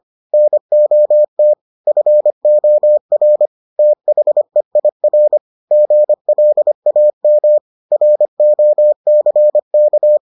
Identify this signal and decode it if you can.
Morse code, heard as a keyed tone, about 25 wpm, 600 Hz. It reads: NOT FOR THEIR GLAM ROCK